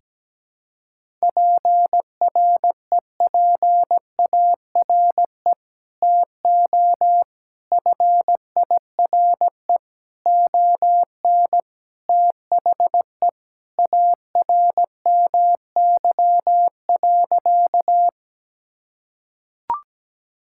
Morse code, heard as a keyed tone, about 17 wpm, 700 Hz. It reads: PREPARE TO FIRE ON THE ARMY.
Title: Sentences from Top 1000 Words - Review Part 6 - 17wpm